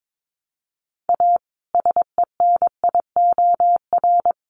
Morse code, heard as a keyed tone, about 22 wpm, 700 Hz. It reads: A SENIOR
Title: Sets of 5 Words Encore 22wpm